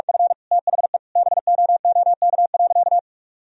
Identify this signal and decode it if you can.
Morse code, heard as a keyed tone, about 46 wpm, 700 Hz. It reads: F THE BOOK.
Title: Sentences from Top 1000 Words - Review Part 1 - 45wpm